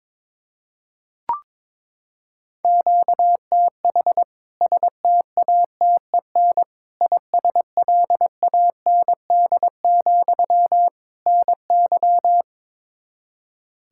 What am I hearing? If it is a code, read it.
Morse code, heard as a keyed tone, about 22 wpm, 700 Hz. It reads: QTH STATEN ISLAND, NY
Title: K1USN SST QSO Practice 22wpm